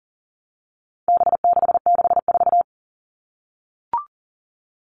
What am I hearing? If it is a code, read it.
Morse code, heard as a keyed tone, about 40 wpm, 700 Hz. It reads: B664
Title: All Cut Numbers - 4-Digit 40wpm